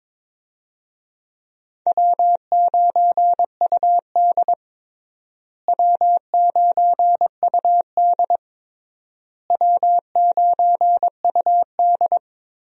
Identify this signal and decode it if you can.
Morse code, heard as a keyed tone, about 22 wpm, 700 Hz. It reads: W9UD W9UD W9UD
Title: USA Call Signs - 4 Characters - Repeated 3x - 22wpm